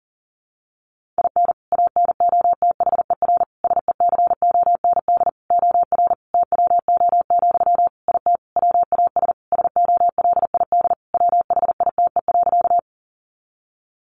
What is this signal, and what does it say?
Morse code, heard as a keyed tone, about 40 wpm, 700 Hz. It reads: IN ANOTHER SECOND OR TWO, IT WAS SOLID WHITE.
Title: Sentences from Top 1500 Words 40wpm